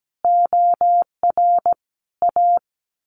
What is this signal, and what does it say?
Morse code, heard as a keyed tone, about 17 wpm, 700 Hz. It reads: OR A
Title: A Sound Of Thunder 17wpm